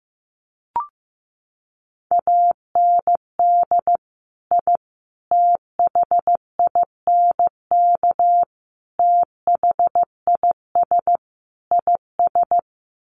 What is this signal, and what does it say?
Morse code, heard as a keyed tone, about 15 wpm, 700 Hz. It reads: AND I THINK THIS IS
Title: Sets of 5 Words Encore 15wpm